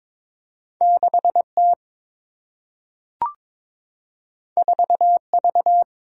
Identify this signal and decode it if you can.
Morse code, heard as a keyed tone, about 22 wpm, 700 Hz. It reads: 6 T 4 V
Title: Single-Letter Single-Number 22wpm